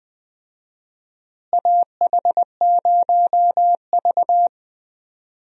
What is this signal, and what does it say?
Morse code, heard as a keyed tone, about 20 wpm, 700 Hz. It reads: AH0V
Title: USA Call Signs - 4 Characters - Repeated 3x - 20wpm